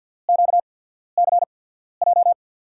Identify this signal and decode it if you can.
Morse code, heard as a keyed tone, about 53 wpm, spaced as for 14 wpm, 700 Hz. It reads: QCJ
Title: USA Call Signs - 5 Characters - 8x Character Spacing - 50wpm